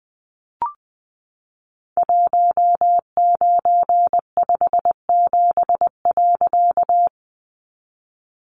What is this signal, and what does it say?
Morse code, heard as a keyed tone, about 20 wpm, 700 Hz. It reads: 1957.
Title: A Sound Of Thunder 20wpm